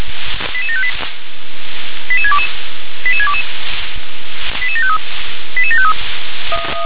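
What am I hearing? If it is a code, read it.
Five-tone selective call (ZVEI), heard as quick runs of five tones, tones of about 70 ms. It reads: 88758 88622 88622 88753 88753
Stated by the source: ZVEI five-tone call received by SDR.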